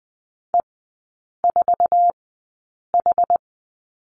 Morse code, heard as a keyed tone, about 20 wpm, spaced as for 8 wpm, 700 Hz. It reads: E4H
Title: 060 Lesson 11e - Letter h - Three Characters 20wpm